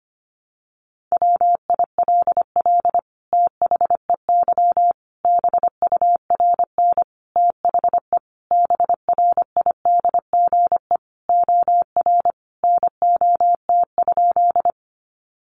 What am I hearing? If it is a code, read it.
Morse code, heard as a keyed tone, about 25 wpm, 700 Hz. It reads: WILL THEY BURN THE BRIDGE OR NOT?
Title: Sentences from Top 1300 Words 25wpm